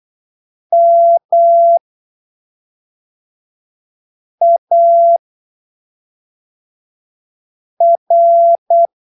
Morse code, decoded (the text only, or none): MAR